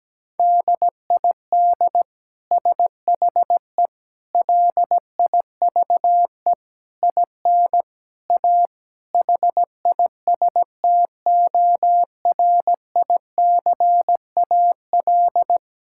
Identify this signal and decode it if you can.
Morse code, heard as a keyed tone, about 17 wpm, 700 Hz. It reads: DID SHE LIVE IN A HISTORICAL
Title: Sentences from Top 1000 Words - Review Part 5 - 17wpm